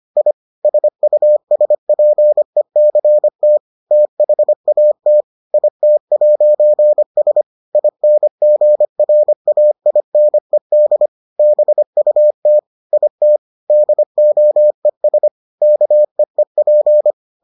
Morse code, heard as a keyed tone, about 25 wpm, 600 Hz. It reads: I SUSPECT THAT IT'S INGRAINED BUT IT DOES KEEP